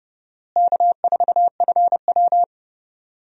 Morse code, heard as a keyed tone, about 29 wpm, 700 Hz. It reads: K4FW
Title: USA Call Signs - 4 Characters - Repeated 2x - 30wpm